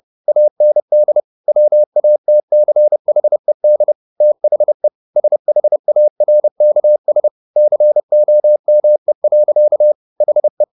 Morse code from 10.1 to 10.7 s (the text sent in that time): HE